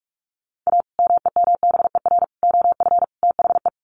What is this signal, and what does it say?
Morse code, heard as a keyed tone, about 44 wpm, 700 Hz. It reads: A MEMBER OF THE
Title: Sets of 4 Words 45wpm